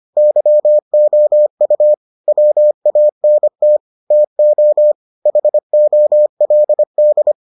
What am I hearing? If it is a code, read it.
Morse code, heard as a keyed tone, about 25 wpm, 600 Hz. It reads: YOU WANT TO HOLD